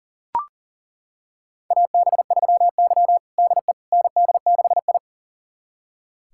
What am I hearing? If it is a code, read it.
Morse code, heard as a keyed tone, about 40 wpm, 700 Hz. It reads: AB3Y DE ND6I
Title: USA Callsign DE Callsign - 4 Characters - 40wpm